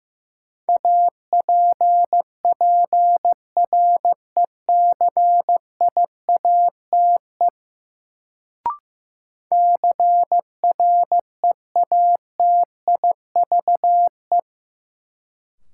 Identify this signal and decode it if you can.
Morse code, heard as a keyed tone, about 15 wpm, 700 Hz. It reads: APPRECIATE CREATIVE
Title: Sentences from Top 1300 Words - New Word Drill Alt - 15wpm